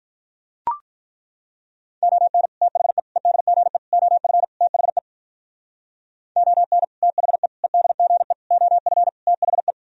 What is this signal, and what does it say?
Morse code, heard as a keyed tone, about 43 wpm, 700 Hz. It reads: ON THE EDGE OF THE ON THE EDGE OF THE
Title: Speed-Racing - Sets of 5 Words 35wpm